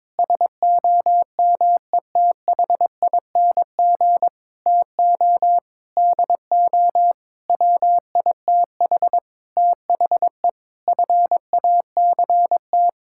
Morse code, heard as a keyed tone, about 22 wpm, 700 Hz. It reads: SOMETHING TO DO WITH THE FACT
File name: practice / 22wpm / Sets of 8 Words - 22wpm